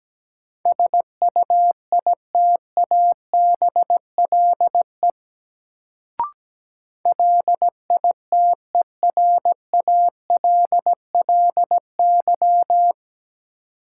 Morse code, heard as a keyed tone, about 17 wpm, 700 Hz. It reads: SUITABLE LITERALLY
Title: Sentences from Top 1400 Words - New Word Drill - 17wpm